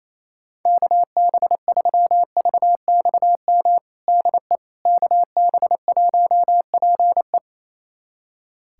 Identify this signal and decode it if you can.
Morse code, heard as a keyed tone, about 27 wpm, 700 Hz. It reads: KB3VXM DE KB1PE
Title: USA Callsign DE Callsign 28wpm